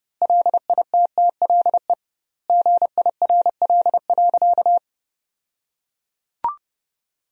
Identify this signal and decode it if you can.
Morse code, heard as a keyed tone, about 32 wpm, 700 Hz. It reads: LITTLE GIRL.
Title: 2x Word-Spacing - Sentences from Top 400 Words - 30wpm